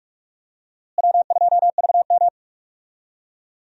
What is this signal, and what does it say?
Morse code, heard as a keyed tone, about 46 wpm, 700 Hz. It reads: W2VM
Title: USA Call Signs - 4 Characters - Repeated 3x - 45wpm